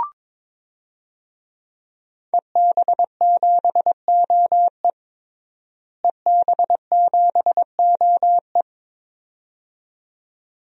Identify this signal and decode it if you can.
Morse code, heard as a keyed tone, about 22 wpm, 700 Hz. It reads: EB7OE EB7OE
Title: DXCC-100 Call Signs - 5 Characters - Repeated 2x - 22wpm